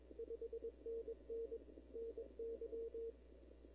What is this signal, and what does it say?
Morse code, heard as a keyed tone, about 22 wpm, 470 Hz. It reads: 5NN NY
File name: CW / 2005-k5tr arrl cw dx -19:00